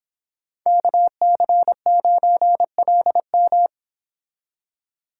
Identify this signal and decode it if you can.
Morse code, heard as a keyed tone, about 26 wpm, 700 Hz. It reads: KC9LM
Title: Speed Racing - USA 2x2 Call Signs 22wpm